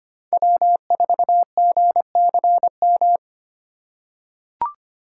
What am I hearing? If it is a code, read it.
Morse code, heard as a keyed tone, about 25 wpm, 700 Hz. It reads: W4GCM
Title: USA Call Signs - 5 Characters - 4x Character Spacing - 25wpm